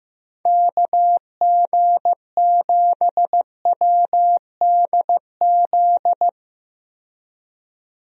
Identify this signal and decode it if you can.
Morse code, heard as a keyed tone, about 15 wpm, 700 Hz. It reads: KG7WDZ
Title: USA Call Signs - 6 Characters - 15wpm